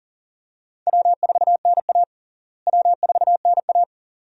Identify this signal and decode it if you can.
Morse code, heard as a keyed tone, about 40 wpm, 700 Hz. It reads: W4NA W4NA
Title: USA Call Signs - 4 Characters - Repeated 2x - 40wpm